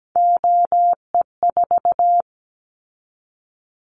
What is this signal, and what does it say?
Morse code, heard as a keyed tone, about 17 wpm, 700 Hz. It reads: OE4
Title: Callsign Prefixes for Top 25 DXCC - 17wpm